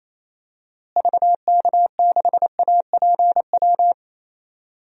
Morse code, heard as a keyed tone, about 28 wpm, 700 Hz. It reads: VK6APW
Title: DXCC-100 Call Signs - 6 Characters - Repeated 3x - 28wpm